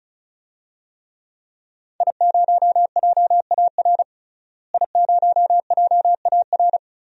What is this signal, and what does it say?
Morse code, heard as a keyed tone, about 34 wpm, 700 Hz. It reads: I0JAR I0JAR
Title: DXCC-100 Call Signs - 5 Characters - Repeated 3x - 35wpm